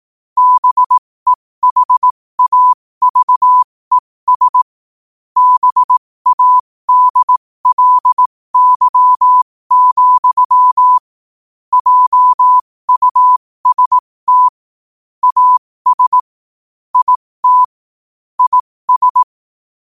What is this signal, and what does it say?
Morse code, heard as a keyed tone, about 18 wpm, spaced as for 15 wpm, 1000 Hz. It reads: BEHAVES BADLY, JUST AS IT IS
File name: qotdfiles / 15wpm